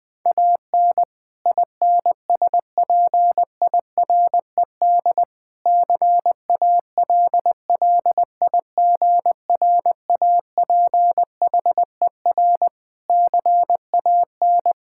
Morse code, heard as a keyed tone, about 20 wpm, 700 Hz. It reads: AN INSPIRED CALLIGRAPHER CAN
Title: Pangram Challenge v2 - 20wpm